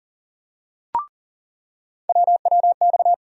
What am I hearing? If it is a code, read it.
Morse code, heard as a keyed tone, about 40 wpm, 700 Hz. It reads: WWX
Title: USA Call Sign Postfixes - 3 Letters 40wpm